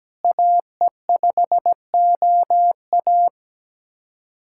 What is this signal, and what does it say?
Morse code, heard as a keyed tone, about 17 wpm, 700 Hz. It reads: AE5OA
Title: USA Callsign DE Callsign 17wpm